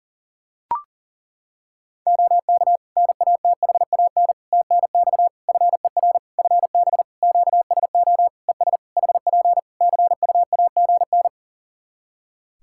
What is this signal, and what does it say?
Morse code, heard as a keyed tone, about 40 wpm, 700 Hz. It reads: OK NATHAN TNX FER FB QSO ES HP CUAGN
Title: Practice QSO - Exchange 4 - Ending - 40wpm